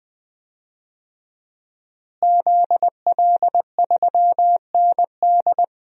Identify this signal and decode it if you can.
Morse code, heard as a keyed tone, about 20 wpm, 700 Hz. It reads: ZL3ND
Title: DXCC-100 Call Signs - 5 Characters - Repeated 3x - 20wpm